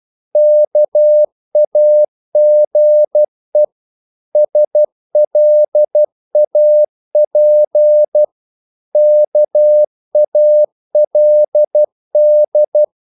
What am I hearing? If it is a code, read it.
Morse code, heard as a keyed tone, about 12 wpm, 600 Hz. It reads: KAGE SLAP KALD
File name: Korte ord DK 12wpm